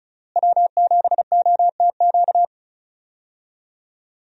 Morse code, heard as a keyed tone, about 34 wpm, 700 Hz. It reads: W7OTQ
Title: USA Call Signs - 5 Characters - Repeated 2x - 35wpm